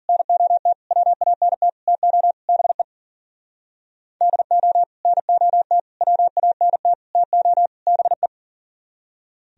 Morse code, heard as a keyed tone, about 44 wpm, 700 Hz. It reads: NOT WANT TO BE DO NOT WANT TO BE
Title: Speed-Racing - Sets of 5 Words 40wpm